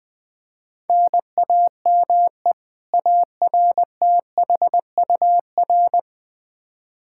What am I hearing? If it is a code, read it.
Morse code, heard as a keyed tone, about 20 wpm, 700 Hz. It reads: NAME ARTHUR
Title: Practice QSO - Exchange 1 - Beginning - 20wpm